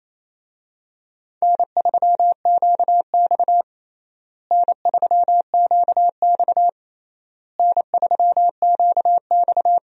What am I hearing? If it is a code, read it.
Morse code, heard as a keyed tone, about 28 wpm, 700 Hz. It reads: N3QX N3QX N3QX
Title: USA Call Signs - 4 Characters - Repeated 3x - 28wpm